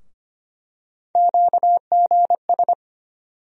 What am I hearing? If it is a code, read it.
Morse code, heard as a keyed tone, about 25 wpm, 700 Hz. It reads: QGS
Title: USA Call Sign Postfixes - 3 Letters 25wpm